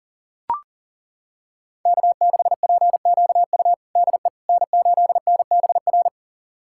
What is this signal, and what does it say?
Morse code, heard as a keyed tone, about 40 wpm, 700 Hz. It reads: K6PQU DE N8NBR
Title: USA Callsign DE Callsign - 5 Characters - 40wpm